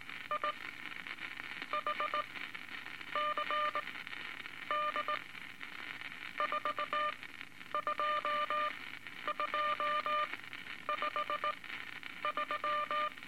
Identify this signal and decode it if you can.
Morse code, heard as a keyed tone, about 19 wpm, spaced as for 9 wpm, 1200 Hz, with noise. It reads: I HCD 42253
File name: Morse Code Records / 17A LCM 162 Eight Words Per Minute